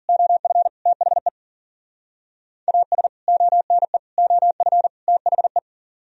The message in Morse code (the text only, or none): OF THE AS ONE OF THE